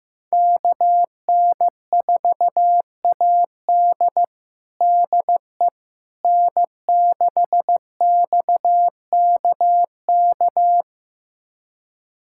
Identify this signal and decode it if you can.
Morse code, heard as a keyed tone, about 15 wpm, 700 Hz. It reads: KN4AD DE N6XKK
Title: USA Callsign DE Callsign - 5 Characters - 15wpm